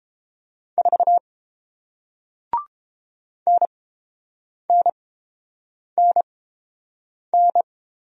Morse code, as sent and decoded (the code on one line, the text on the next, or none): ....- / -. -. -. -.
4 NNNN